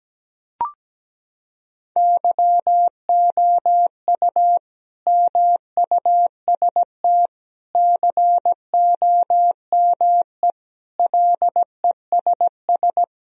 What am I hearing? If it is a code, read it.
Morse code, heard as a keyed tone, about 17 wpm, 700 Hz. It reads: YOU MUST COME LESS
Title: Sentences from Top 1000 Words - Review Part 2 - 17wpm